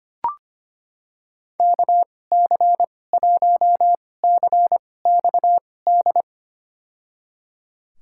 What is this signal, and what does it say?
Morse code, heard as a keyed tone, about 25 wpm, spaced as for 18 wpm, 700 Hz. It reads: KC1CXD
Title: USA Call Signs - 6 Characters - 2x Character Spacing - 25wpm